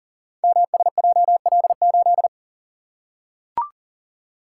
Morse code, heard as a keyed tone, about 40 wpm, 690 Hz. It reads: MSJL8
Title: Code Groups - 5 Letter:Numbers - 40wpm